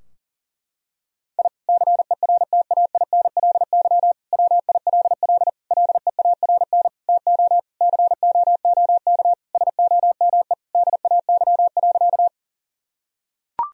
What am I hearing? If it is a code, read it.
Morse code, heard as a keyed tone, about 40 wpm, 700 Hz. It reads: I CERTAINLY WILL LEARN TO COOK SOME DAY.
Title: Sentences from Top 1000 Words - Review Part 4 - 40wpm